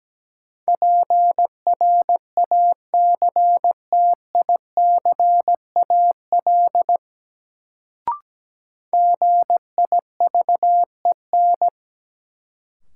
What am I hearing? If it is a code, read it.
Morse code, heard as a keyed tone, about 17 wpm, 700 Hz. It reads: PRACTICAL GIVEN
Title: Sentences from Top 1400 Words - New Word Drill - 17wpm